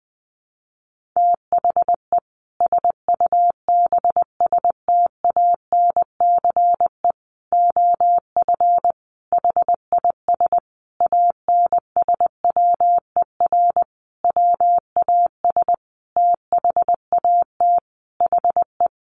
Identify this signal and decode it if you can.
Morse code, heard as a keyed tone, about 19 wpm, 700 Hz. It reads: THE SUBSTANCE OF HIS ANSWER WAS THAT HE
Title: Sentences from Top 1500 Words 20wpm